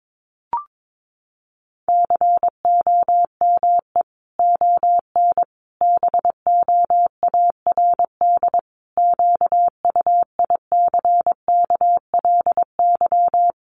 Morse code, heard as a keyed tone, about 22 wpm, 700 Hz. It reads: COME ON BOARD QUICKLY
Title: Sentences from Top 1300 Words 22wpm